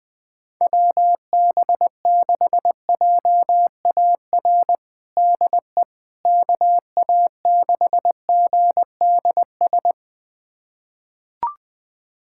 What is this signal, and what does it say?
Morse code, heard as a keyed tone, about 21 wpm, 700 Hz. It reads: WB6JAR DE KA6GDS
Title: USA Callsign DE Callsign - 6 Characters - 20wpm